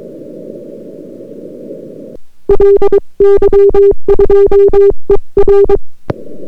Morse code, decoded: LY2ER